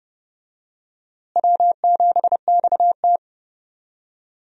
Morse code, heard as a keyed tone, about 30 wpm, 700 Hz. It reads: W7XT
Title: USA Call Signs - 4 Characters - Repeated 2x - 30wpm